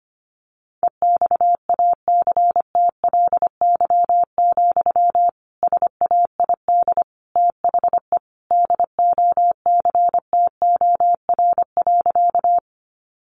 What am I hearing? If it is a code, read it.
Morse code, heard as a keyed tone, about 25 wpm, 700 Hz. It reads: EXACTLY, SAID THE DOCTOR.
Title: Sentences from Top 1500 Words 25wpm